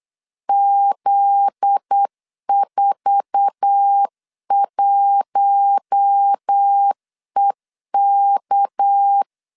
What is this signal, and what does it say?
Morse code, heard as a keyed tone, about 8 wpm, 800 Hz. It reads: Z41EK
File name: LEZIONI / 9 286 8wpm 3425